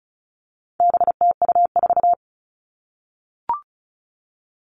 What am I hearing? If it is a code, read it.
Morse code, heard as a keyed tone, about 34 wpm, 700 Hz. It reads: BTU4
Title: All Cut Numbers - 4-Digit 35wpm